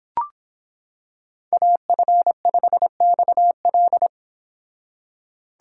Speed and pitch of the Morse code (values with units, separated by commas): 25 wpm, 700 Hz